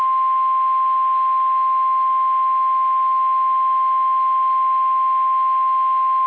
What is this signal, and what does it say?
Emergency Alert System (SAME) tones (the 1050 Hz attention tone). It is heard from 1.1 to 5.6 s.
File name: eas / KSDZKDJL / KSDZKDJL